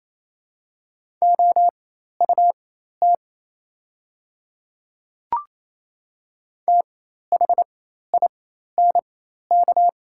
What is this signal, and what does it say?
Morse code, heard as a keyed tone, about 28 wpm, spaced as for 13 wpm, 700 Hz. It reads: OUT THINK
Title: 4x Character Spacing - Top 200 Words - 28wpm